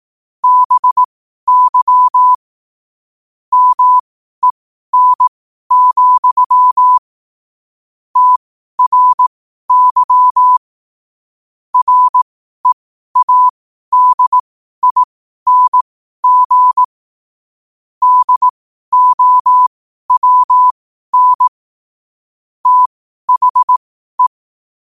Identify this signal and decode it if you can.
Morse code, heard as a keyed tone, about 18 wpm, spaced as for 12 wpm, 1000 Hz. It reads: BY MEN, TRY READING DOWN THE
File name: qotdfiles / 12wpm